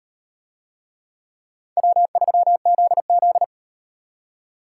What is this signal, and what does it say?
Morse code, heard as a keyed tone, about 34 wpm, 700 Hz. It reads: W3ZZ